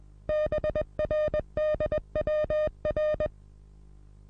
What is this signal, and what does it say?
Morse code, heard as a keyed tone, about 20 wpm, 610 Hz. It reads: BRDWR
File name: repos / Morsekurs / DJ4UF Morsen26